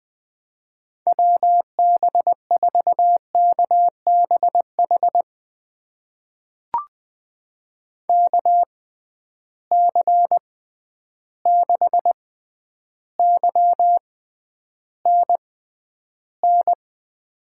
Morse code, heard as a keyed tone, about 20 wpm, spaced as for 10 wpm, 700 Hz. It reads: WB4KBH KC6YNN
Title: USA Call Signs - 6 Characters - 6x Character Spacing - 20wpm